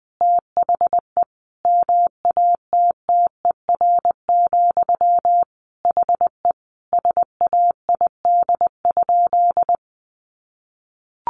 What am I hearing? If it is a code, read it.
Morse code, heard as a keyed tone, about 20 wpm, 700 Hz. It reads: THE MATTER, HE SAID?
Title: Sentences from Top 700 Words 20wpm